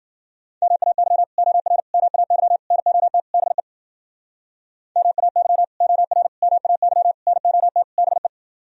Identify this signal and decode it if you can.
Morse code, heard as a keyed tone, about 52 wpm, 700 Hz. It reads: MAY OR MAY NOT BE MAY OR MAY NOT BE